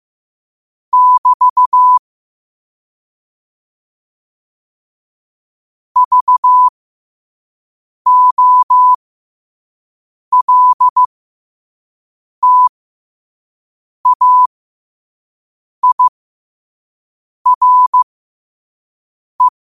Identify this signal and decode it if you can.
Morse code, heard as a keyed tone, about 15 wpm, spaced as for 5 wpm, 1000 Hz. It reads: <BT> VOLTAIRE
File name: qotdfiles / 5wpm